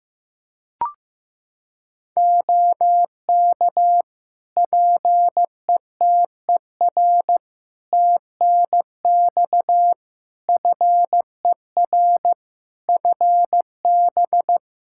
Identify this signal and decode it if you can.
Morse code, heard as a keyed tone, about 15 wpm, 700 Hz. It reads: OK PETER TNX FER FB
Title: K1USN SST QSO Practice 15wpm